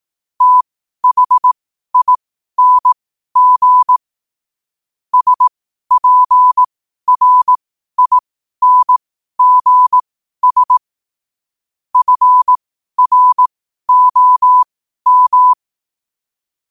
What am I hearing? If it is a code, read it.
Morse code, heard as a keyed tone, about 18 wpm, spaced as for 12 wpm, 1000 Hz. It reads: THING SPRINGS FROM